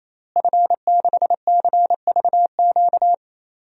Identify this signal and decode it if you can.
Morse code, heard as a keyed tone, about 28 wpm, 700 Hz. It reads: F6CVQ
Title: DXCC-100 Call Signs - 5 Characters - Repeated 3x - 28wpm